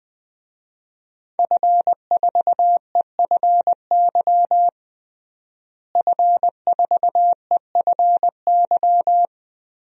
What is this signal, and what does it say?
Morse code, heard as a keyed tone, about 20 wpm, 700 Hz. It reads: F4EFY F4EFY
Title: DXCC-100 Call Signs - 5 Characters - Repeated 3x - 20wpm